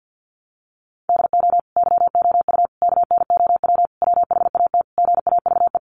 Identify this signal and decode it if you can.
Morse code, heard as a keyed tone, about 50 wpm, 700 Hz. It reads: DO YOU KNOW WHAT GAVE
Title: Sentences from Top 1300 Words 50wpm